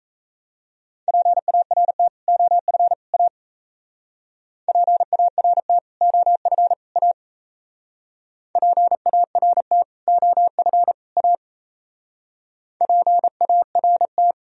Morse code, heard as a keyed tone, about 37 wpm, 700 Hz. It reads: PART OF A PART OF A PART OF A PART